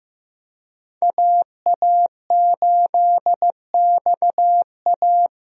Morse code, heard as a keyed tone, about 15 wpm, 700 Hz. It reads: AA8XA
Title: USA Call Signs - 5 Characters - Repeated 3x - 15wpm